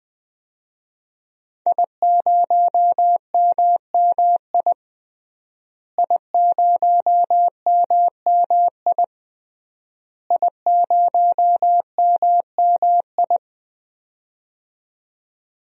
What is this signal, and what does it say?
Morse code, heard as a keyed tone, about 20 wpm, 700 Hz. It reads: I0MMI I0MMI I0MMI
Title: DXCC-100 Call Signs - 5 Characters - Repeated 3x - 20wpm